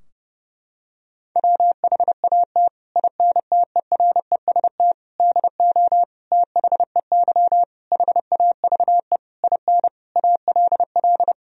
Morse code, heard as a keyed tone, about 32 wpm, 700 Hz. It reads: WHAT INTEREST DO THEY HAVE IN ALL
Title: Sentences from Top 1000 Words - Review Part 2 - 30wpm